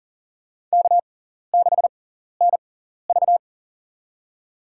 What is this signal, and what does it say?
Morse code, heard as a keyed tone, about 40 wpm, spaced as for 14 wpm, 700 Hz. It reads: K6NV
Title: USA Call Signs - 4 Characters - 6x Character Spacing - 40wpm